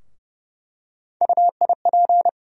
Morse code, heard as a keyed tone, about 31 wpm, 700 Hz. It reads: UIP
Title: USA Call Sign Postfixes - 3 Letters 30wpm